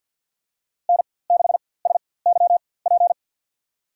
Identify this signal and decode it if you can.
Morse code, heard as a keyed tone, about 53 wpm, spaced as for 24 wpm, 700 Hz. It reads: N6SYP
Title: USA Call Signs - 5 Characters - 4x Character Spacing - 50wpm